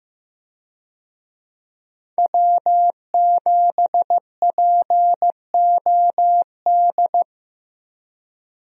Morse code, heard as a keyed tone, about 15 wpm, 700 Hz. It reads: W7POD
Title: USA Call Signs - 5 Characters - 15wpm